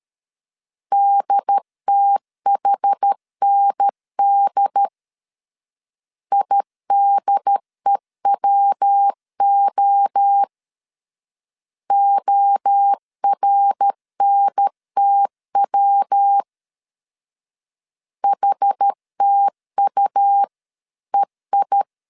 Morse code, heard as a keyed tone, about 13 wpm, 800 Hz. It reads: DTHND IDEWO ORNTW HTUEI
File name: CW / DET / 1 3 1500C S B 12wpm 2828